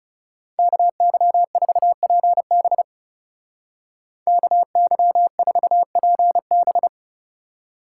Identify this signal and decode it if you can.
Morse code, heard as a keyed tone, about 38 wpm, 700 Hz. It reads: KY4PB KY4PB